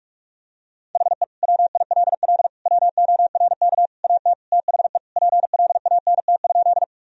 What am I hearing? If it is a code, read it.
Morse code, heard as a keyed tone, about 41 wpm, 700 Hz. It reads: HE WILL WORK AT THE PLANT?
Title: Sentences from Top 1000 Words - Review Part 1 - 45wpm